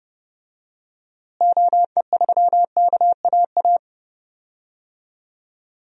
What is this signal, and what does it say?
Morse code, heard as a keyed tone, about 30 wpm, 700 Hz. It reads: OE3KAA
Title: DXCC-100 Call Signs - 6 Characters - Repeated 2x - 30wpm